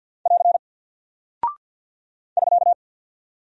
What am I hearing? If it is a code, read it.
Morse code, heard as a keyed tone, about 50 wpm, 700 Hz. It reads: <AR> <SK>
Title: Single Character (Letter:Number:Punctuation:Prosign) 50wpm